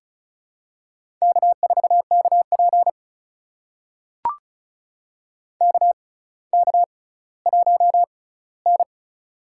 Morse code, heard as a keyed tone, about 34 wpm, spaced as for 18 wpm, 700 Hz. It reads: K4KP KK1N
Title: USA Call Signs - 4 Characters - 6x Character Spacing - 35wpm